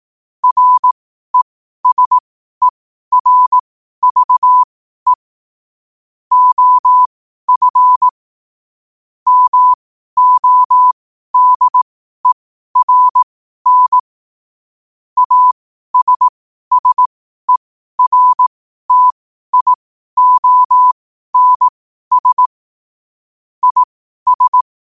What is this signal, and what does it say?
Morse code, heard as a keyed tone, about 18 wpm, spaced as for 12 wpm, 1000 Hz. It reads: RESERVE OF MODERN ASSERTIONS IS
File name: qotdfiles / 12wpm